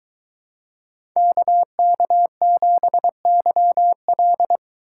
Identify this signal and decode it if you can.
Morse code, heard as a keyed tone, about 22 wpm, 700 Hz. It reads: KK7YL